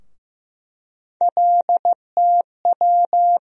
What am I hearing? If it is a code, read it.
Morse code, heard as a keyed tone, about 15 wpm, 700 Hz. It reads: LTW
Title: USA Call Sign Postfixes - 3 Letters 15wpm